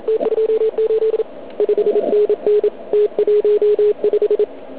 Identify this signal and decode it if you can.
Morse code, heard as a keyed tone, about 36 wpm, 440 Hz. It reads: T28 5NN T15